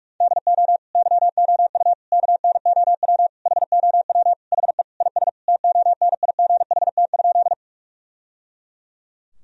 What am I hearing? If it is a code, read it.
Morse code, heard as a keyed tone, about 41 wpm, 700 Hz. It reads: DO YOU KNOW HOW HE IS TONIGHT?
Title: Sentences from Top 1000 Words - Review Part 3 - 45wpm